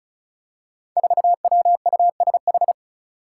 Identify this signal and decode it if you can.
Morse code, heard as a keyed tone, about 36 wpm, 700 Hz. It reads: 4WUSH
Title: Five Challenging Letter-Number Combinations 35wpm